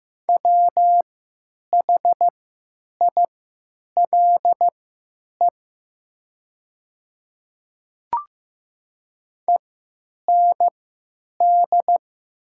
Morse code, heard as a keyed tone, about 15 wpm, spaced as for 9 wpm, 700 Hz. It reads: WHILE END
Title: 3x Character Spacing - Top 300 Words - 15wpm